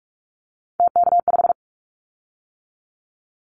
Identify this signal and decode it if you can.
Morse code, heard as a keyed tone, about 44 wpm, 700 Hz. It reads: TK5
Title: Callsign Prefixes for Top 100 DXCC - 45wpm